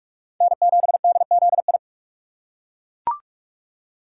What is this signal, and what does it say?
Morse code, heard as a keyed tone, about 41 wpm, 700 Hz. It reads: N7DZI
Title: USA Call Signs - 5 Characters - 3x Character Spacing - 45wpm